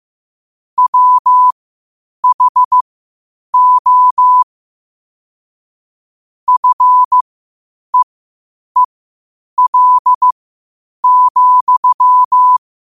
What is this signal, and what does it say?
Morse code, heard as a keyed tone, about 15 wpm, spaced as for 8 wpm, 1000 Hz. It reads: WHO FEEL,
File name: qotdfiles / 8wpm